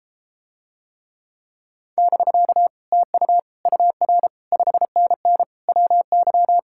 Morse code, heard as a keyed tone, about 33 wpm, 700 Hz. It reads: <BK> TU UR 5NN WY